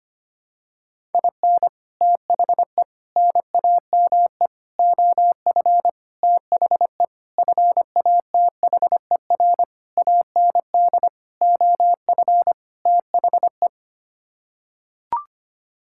Morse code, heard as a keyed tone, about 25 wpm, 700 Hz. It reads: IN THE NAME OF THE FATHER AND OF THE
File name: Sets of 9 Words - 25wpm